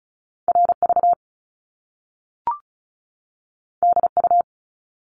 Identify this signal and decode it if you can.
Morse code, heard as a keyed tone, about 36 wpm, 700 Hz. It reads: RV DU
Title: Top Two Letter Combinations in Words 35wpm